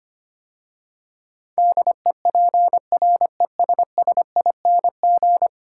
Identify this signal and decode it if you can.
Morse code, heard as a keyed tone, about 25 wpm, 700 Hz. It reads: DEPRESSING
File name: 10 Letter Words - First 3 Letters Given - 25wpm